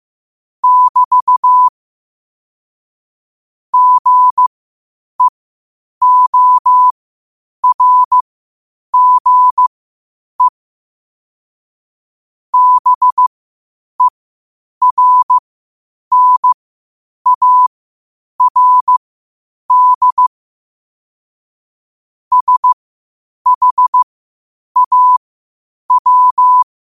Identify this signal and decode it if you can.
Morse code, heard as a keyed tone, about 15 wpm, spaced as for 8 wpm, 1000 Hz. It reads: <BT> GEORGE BERNARD SHAW
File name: qotdfiles / 8wpm